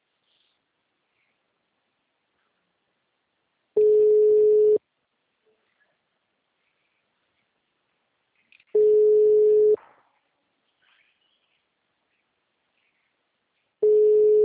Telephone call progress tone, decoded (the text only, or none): ringback tone